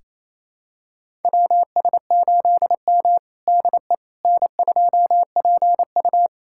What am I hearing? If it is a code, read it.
Morse code, heard as a keyed tone, about 28 wpm, 700 Hz. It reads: WS8M DE N2PU
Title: USA Callsign DE Callsign - 4 Characters - 28wpm